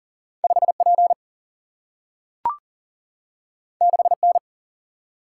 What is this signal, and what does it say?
Morse code, heard as a keyed tone, about 41 wpm, 700 Hz. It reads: 5P 6N
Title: Single-Letter Single-Number 40wpm